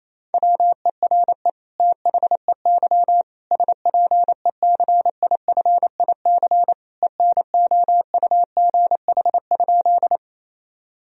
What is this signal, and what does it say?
Morse code, heard as a keyed tone, about 28 wpm, 700 Hz. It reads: WERE THEY SPECIFIC ENOUGH?
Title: Sentences from Top 1000 Words - Review Part 3 - 28wpm